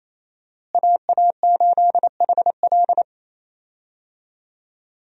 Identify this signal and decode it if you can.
Morse code, heard as a keyed tone, about 28 wpm, 700 Hz. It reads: AA8HL
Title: USA Call Signs - 5 Characters - Repeated 3x - 28wpm